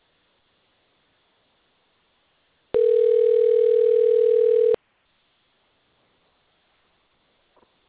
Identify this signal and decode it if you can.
Ringback tone (North America), 440 + 480 Hz.